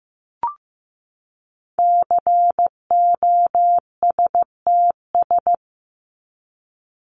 Morse code, heard as a keyed tone, about 15 wpm, 700 Hz. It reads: COSTS
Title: ES:S Suffix - Top 100 Words with Base Words 15wpm